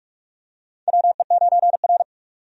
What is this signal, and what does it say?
Morse code, heard as a keyed tone, about 42 wpm, 700 Hz. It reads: WE9R
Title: USA Call Signs - 4 Characters - 45wpm